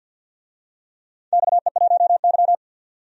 Morse code, heard as a keyed tone, about 53 wpm, 700 Hz. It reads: XE1Y
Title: DXCC-100 Call Signs - 4 Characters - 50wpm